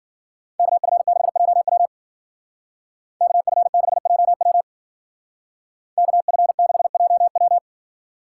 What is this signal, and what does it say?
Morse code, heard as a keyed tone, about 49 wpm, 700 Hz. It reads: KF6JW KF6JW KF6JW
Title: Speed Racing - USA 2x2 Call Signs 40wpm